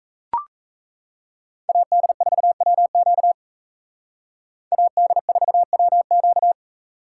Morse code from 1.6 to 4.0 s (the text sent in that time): AD4WQ